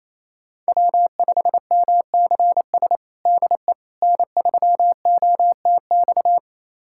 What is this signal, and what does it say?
Morse code, heard as a keyed tone, about 28 wpm, 700 Hz. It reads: W5MCS DE N3OTX
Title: USA Callsign DE Callsign - 5 Characters - 28wpm